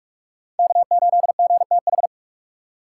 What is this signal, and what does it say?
Morse code, heard as a keyed tone, about 58 wpm, 700 Hz. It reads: K8GTH